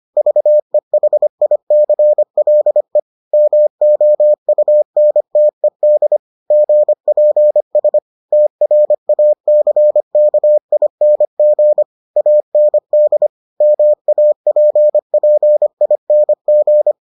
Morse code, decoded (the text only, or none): VEHICLE MOUNTED GPS TRACKING AND MAPPING